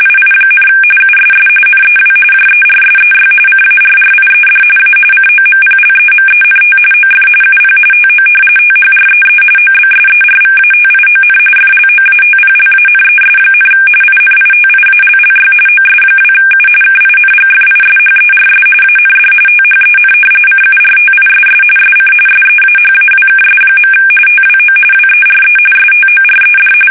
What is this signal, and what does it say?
RTTY, about 45.45 baud. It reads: II, USS NEW JERSEY CARRIED A COMPLEMENT OF UP TO THREE OBSERVATION FLOATPLANES, INITIALLY OS2U KINGFISHERS AND THEN CURTISS SC SEAHAWKS. TWO WERE STORED
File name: fsk850 / battleshipnj